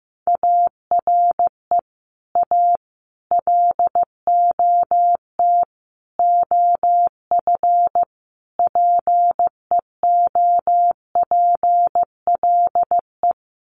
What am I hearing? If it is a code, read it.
Morse code, heard as a keyed tone, about 15 wpm, 700 Hz. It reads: ARE A LOT OF PEOPLE
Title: Sets of 5 Words 15wpm